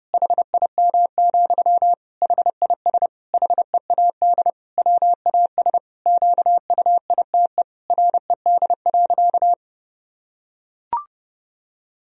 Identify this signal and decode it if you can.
Morse code, heard as a keyed tone, about 32 wpm, 700 Hz. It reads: HIM, HIS HEAD WAS QUITE RED.
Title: Sentences from Top 1000 Words - Review Part 2 - 30wpm